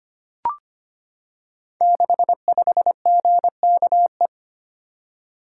Morse code, heard as a keyed tone, about 25 wpm, 700 Hz. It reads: 65GKE
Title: Code Groups - 5 Letter:Numbers - 25wpm